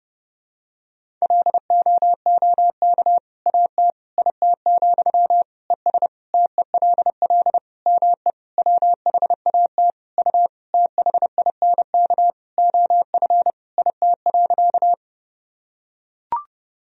Morse code, heard as a keyed tone, about 32 wpm, 700 Hz. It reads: LOOK AT IT, ES TELL ME WHAT U THINK OF IT.
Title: Sentences from Top 200 Words and Abbreviations 30wpm